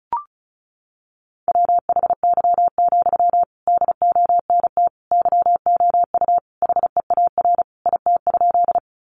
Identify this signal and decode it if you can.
Morse code, heard as a keyed tone, about 34 wpm, 700 Hz. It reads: WHY, DONT YOU HEAR IT?
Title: Sentences from Top 300 Words 35wpm